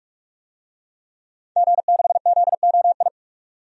Morse code, heard as a keyed tone, about 46 wpm, 700 Hz. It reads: G6ZOI
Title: DXCC-100 Call Signs - 5 Characters - 45wpm